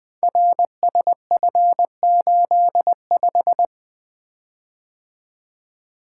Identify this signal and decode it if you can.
Morse code, heard as a keyed tone, about 20 wpm, 700 Hz. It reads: RSF85
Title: Code Groups - 5 Letter:Numbers - 20wpm